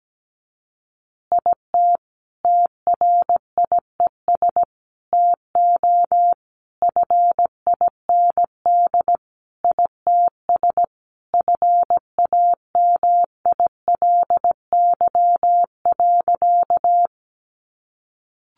Morse code, heard as a keyed tone, about 17 wpm, 700 Hz. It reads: IT TRIES TO FIND ITS FAMILY.
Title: Additional Easy Phrases 17wpm